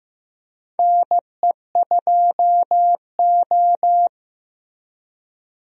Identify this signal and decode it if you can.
Morse code, heard as a keyed tone, about 15 wpm, 700 Hz. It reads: NE2O
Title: USA Call Signs - 4 Characters - Repeated 3x - 15wpm